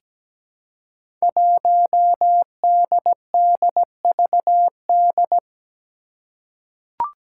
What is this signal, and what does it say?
Morse code, heard as a keyed tone, about 17 wpm, 700 Hz. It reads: 1DDVD
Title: Five Challenging Letter-Number Combinations 17wpm